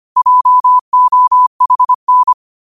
Morse code, heard as a keyed tone, about 25 wpm, 1000 Hz. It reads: JOHN
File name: qotdfiles / 25wpm